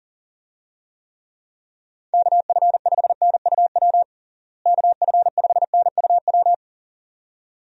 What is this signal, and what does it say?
Morse code, heard as a keyed tone, about 40 wpm, 700 Hz. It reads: KF5NUW KF5NUW
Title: USA Call Signs - 6 Characters - Repeated 2x - 40wpm